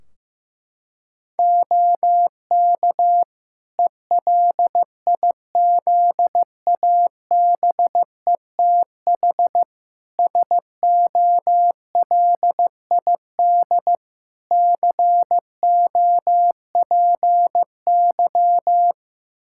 Morse code, heard as a keyed tone, about 15 wpm, 700 Hz. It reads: OK ELIZABETH SOLID COPY
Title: Practice QSO - Exchange 3 - More Info - 15wpm